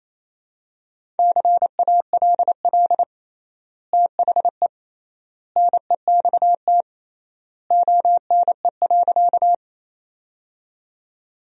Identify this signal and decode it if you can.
Morse code, heard as a keyed tone, about 28 wpm, 700 Hz. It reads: CALL THE NEXT ONE.
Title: 3x Word-Spacing - Sentences from Top 300 Words - 28wpm